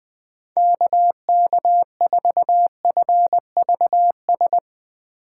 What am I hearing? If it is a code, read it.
Morse code, heard as a keyed tone, about 20 wpm, 700 Hz. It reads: KK4FVS
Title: USA Call Signs - 6 Characters - 3x Character Spacing - 20wpm